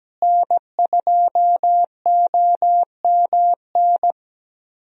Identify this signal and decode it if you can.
Morse code, heard as a keyed tone, about 17 wpm, 700 Hz. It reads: N2OMN
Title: USA Call Signs - 5 Characters - Repeated 3x - 17wpm